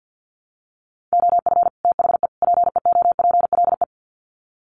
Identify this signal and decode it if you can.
Morse code, heard as a keyed tone, about 50 wpm, 700 Hz. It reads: OF THE PEOPLE
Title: Sets of 3 Words 50wpm